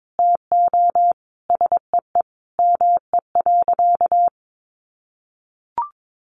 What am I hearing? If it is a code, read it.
Morse code, heard as a keyed tone, about 22 wpm, 700 Hz. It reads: TO SEE ME.
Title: Sentences from Top 1500 Words 22wpm